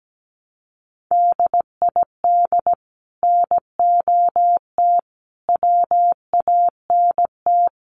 Morse code, heard as a keyed tone, about 17 wpm, 700 Hz. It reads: DID NOT WANT
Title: Sets of 5 Words 17wpm